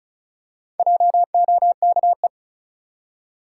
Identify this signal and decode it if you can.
Morse code, heard as a keyed tone, about 36 wpm, 700 Hz. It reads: JOKE
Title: Four Letter Words - 35wpm